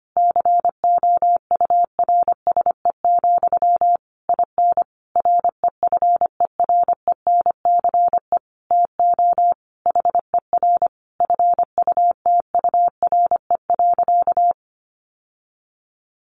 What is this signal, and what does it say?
Morse code, heard as a keyed tone, about 25 wpm, 700 Hz. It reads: COURSE, IN REFERENCE TO HER FUTURE.
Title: Sentences from Top 1200 Words 25wpm